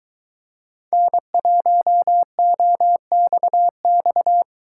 Morse code, heard as a keyed tone, about 23 wpm, 700 Hz. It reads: N1OXX